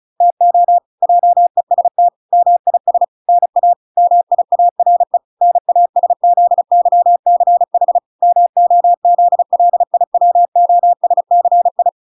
Morse code, heard as a keyed tone, about 34 wpm, 700 Hz. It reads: TO JEST MIS NA MIARE NASZYCH MOZLIWOSCI